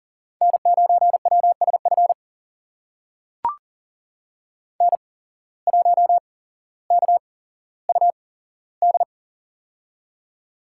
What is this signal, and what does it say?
Morse code, heard as a keyed tone, about 40 wpm, spaced as for 17 wpm, 700 Hz. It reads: N9WSF N1KUD
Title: USA Call Signs - 5 Characters - 8x Character Spacing - 40wpm